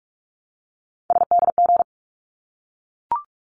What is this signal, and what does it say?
Morse code, heard as a keyed tone, about 46 wpm, 700 Hz. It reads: SDG
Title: ARRL Sections 45wpm